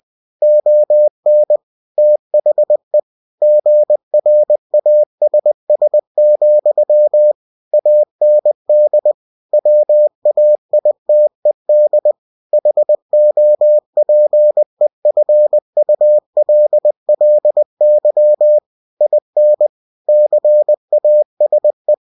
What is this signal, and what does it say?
Morse code, heard as a keyed tone, about 20 wpm, 600 Hz. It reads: ON THE GRASS, AND WAITED HOPEFULLY IN CASE
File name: cw-books / winnie-the-pooh / winnie-20wpm-page-0106